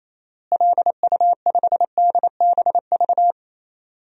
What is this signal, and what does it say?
Morse code, heard as a keyed tone, about 28 wpm, 700 Hz. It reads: LU5DBV
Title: DXCC-100 Call Signs - 6 Characters - Repeated 3x - 28wpm